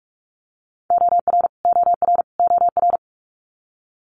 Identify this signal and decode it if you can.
Morse code, heard as a keyed tone, about 46 wpm, 700 Hz. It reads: OR OR OR
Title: Multi-letter Phonemes Repeated 3x 45wpm